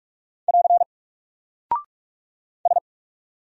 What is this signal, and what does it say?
Morse code, heard as a keyed tone, about 48 wpm, 700 Hz. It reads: <AR> S
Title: Single Character (Letter:Number:Punctuation:Prosign) Alt 45wpm